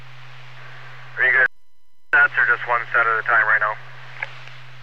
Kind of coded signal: Emergency Alert System (SAME) tones